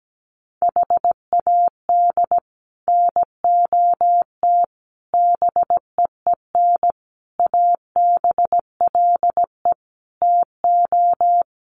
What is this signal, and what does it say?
Morse code, heard as a keyed tone, about 17 wpm, 700 Hz. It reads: HAD NOT BEEN ABLE TO
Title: Sets of 5 Words Encore 17wpm